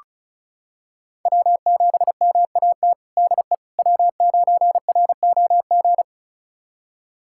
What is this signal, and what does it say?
Morse code, heard as a keyed tone, about 34 wpm, 700 Hz. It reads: W7MAT DE W9ROG
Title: USA Callsign DE Callsign - 5 Characters - 35wpm